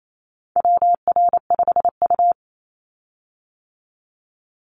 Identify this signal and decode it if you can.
Morse code, heard as a keyed tone, about 28 wpm, 700 Hz. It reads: WR5U
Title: Call Signs - 28wpm